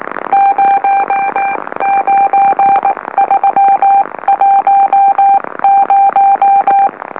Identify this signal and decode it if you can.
Morse code, heard as a keyed tone, about 19 wpm, spaced as for 16 wpm, 790 Hz. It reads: 09310
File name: M04- -2345utc-LOLO